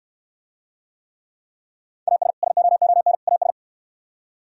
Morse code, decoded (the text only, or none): AS SOON AS